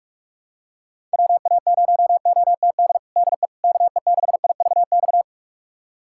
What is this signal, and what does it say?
Morse code, heard as a keyed tone, about 46 wpm, 700 Hz. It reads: WA0OTD DE KE6IVX
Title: USA Callsign DE Callsign 45wpm